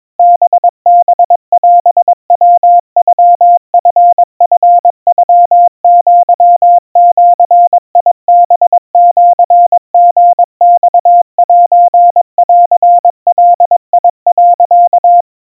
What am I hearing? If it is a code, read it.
Morse code (Japanese Wabun code), heard as a keyed tone, about 22 wpm, 700 Hz. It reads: ハハオヤノチチノアジハシリマセンガ、
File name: 80 cpm 700 Hz Crispy